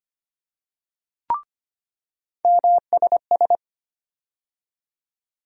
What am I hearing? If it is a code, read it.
Morse code, heard as a keyed tone, about 25 wpm, 700 Hz. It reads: MSS
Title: USA Call Sign Postfixes - 3 Letters 25wpm